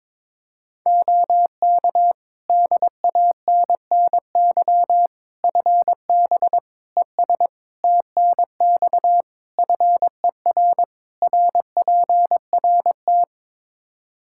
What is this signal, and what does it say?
Morse code, heard as a keyed tone, about 22 wpm, 700 Hz. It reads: OK DANNY FB ES TNX FER RPRT
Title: Practice QSO - Exchange 2 - Info - 22wpm